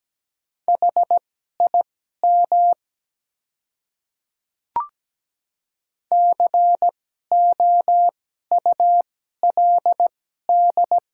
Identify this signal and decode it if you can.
Morse code, heard as a keyed tone, about 17 wpm, spaced as for 12 wpm, 700 Hz. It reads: HIM COULD
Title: 2x Character Spacing - Top 100 Words - 17wpm